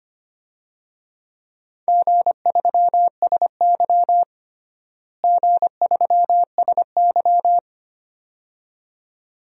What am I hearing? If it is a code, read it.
Morse code, heard as a keyed tone, about 25 wpm, 700 Hz. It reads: G3SY G3SY
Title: DXCC-100 Call Signs - 4 Characters - Repeated 2x - 25wpm